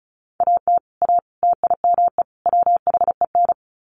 Morse code, heard as a keyed tone, about 34 wpm, 700 Hz. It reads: AT A TIME WHEN
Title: Sets of 4 Words 35wpm